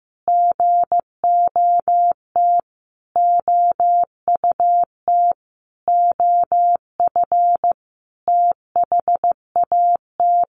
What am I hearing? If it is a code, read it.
Morse code, heard as a keyed tone, about 15 wpm, 700 Hz. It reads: GOT OUT OF THAT
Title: Sentences from Top 1500 Words 15wpm